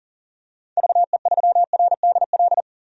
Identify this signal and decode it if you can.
Morse code, heard as a keyed tone, about 40 wpm, 700 Hz. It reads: VE3RDL
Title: DXCC-100 Call Signs - 6 Characters - 40wpm